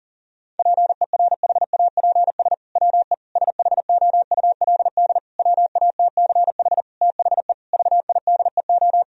Morse code, heard as a keyed tone, about 40 wpm, 700 Hz. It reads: PERHAPS WE SHOULD WATCH THE VIDEO
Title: Sentences from Top 1000 Words - Review Part 4 - 40wpm